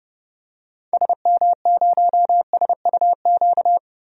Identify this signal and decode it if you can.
Morse code, heard as a keyed tone, about 32 wpm, 700 Hz. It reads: SM0SUQ
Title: DXCC-100 Call Signs - 6 Characters - Repeated 3x - 30wpm